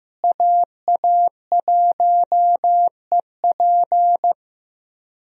Morse code, heard as a keyed tone, about 15 wpm, 700 Hz. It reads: AA1EP
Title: USA Call Signs - 5 Characters - Repeated 2x - 15wpm